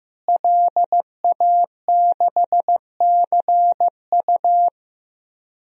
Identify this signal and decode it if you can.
Morse code, heard as a keyed tone, about 15 wpm, 700 Hz. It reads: LA6CU
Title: DXCC-100 Call Signs - 5 Characters - Repeated 3x - 15wpm